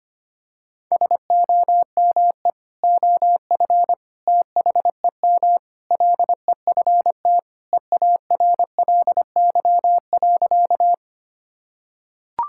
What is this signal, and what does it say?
Morse code, heard as a keyed tone, about 25 wpm, 700 Hz. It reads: SOME OF THEM LEFT EARLY.
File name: practice / 25wpm / Sentences with 3-Word Phrases - 25wpm